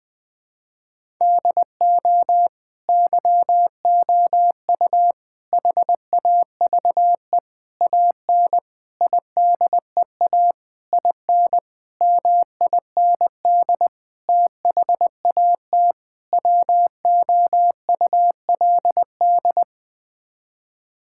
Morse code, heard as a keyed tone, about 20 wpm, 700 Hz. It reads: DO YOU HAVE AN IDEA IN MIND THAT WOULD
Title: Sets of 9 Words - 20wpm